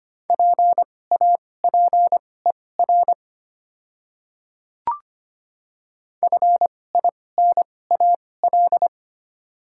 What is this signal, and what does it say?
Morse code, heard as a keyed tone, about 25 wpm, spaced as for 18 wpm, 700 Hz. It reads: PAPER FINAL
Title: 2x Character Spacing - Top 500 Words - 25wpm